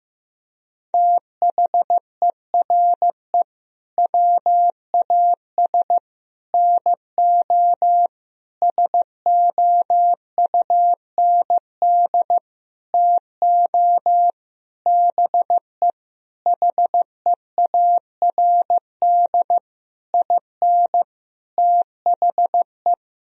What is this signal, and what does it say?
Morse code, heard as a keyed tone, about 15 wpm, 700 Hz. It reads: THERE WAS NO SOUND TO BE HEARD IN THE
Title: Sentences from Top 1000 Words - Review Part 3 - 15wpm